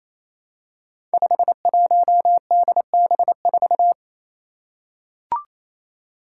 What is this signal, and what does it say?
Morse code, heard as a keyed tone, about 27 wpm, 700 Hz. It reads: 51DB4
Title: Five Challenging Letter-Number Combinations 28wpm